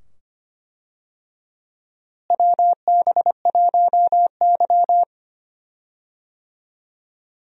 Morse code, heard as a keyed tone, about 25 wpm, 700 Hz. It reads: WB1Y
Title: USA Call Signs - 4 Characters - Repeated 2x - 25wpm